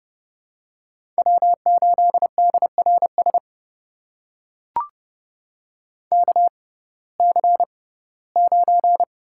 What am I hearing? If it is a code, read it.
Morse code, heard as a keyed tone, about 30 wpm, spaced as for 18 wpm, 700 Hz. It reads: W8DRS KC9
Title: USA Call Signs - 5 Characters - 6x Character Spacing - 30wpm